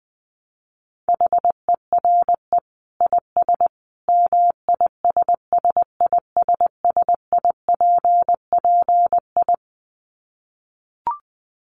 Morse code, heard as a keyed tone, about 20 wpm, 700 Hz. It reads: HERE IS MISSISSIPPI
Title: Dit Challenge 20wpm